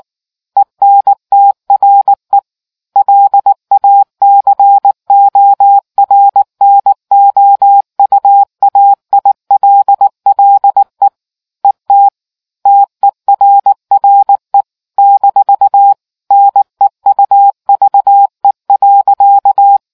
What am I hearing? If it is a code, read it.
Morse code, heard as a keyed tone, about 19 wpm, 800 Hz. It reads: ENTRE LA CORNOUAILLE ET TERRE-NEUVE.